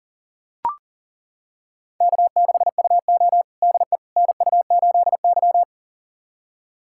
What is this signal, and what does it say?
Morse code, heard as a keyed tone, about 37 wpm, 690 Hz. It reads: K6UO DE NU8Y
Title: USA Callsign DE Callsign - 4 Characters - 40wpm